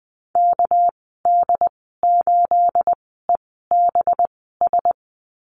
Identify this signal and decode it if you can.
Morse code, heard as a keyed tone, about 20 wpm, spaced as for 15 wpm, 700 Hz. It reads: KD8EBS
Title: 219 Lesson 35g - Number 0 - Call Signs 20wpm